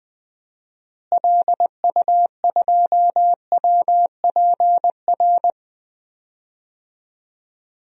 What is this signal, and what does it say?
Morse code, heard as a keyed tone, about 20 wpm, 700 Hz. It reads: LU2WPR
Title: DXCC-100 Call Signs - 6 Characters - Repeated 2x - 20wpm